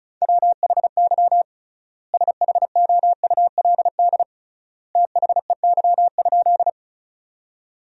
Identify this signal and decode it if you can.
Morse code, heard as a keyed tone, about 36 wpm, 700 Hz. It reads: WHY SHOULD THEY?
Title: 3x Word-Spacing - Sentences from Top 300 Words - 35wpm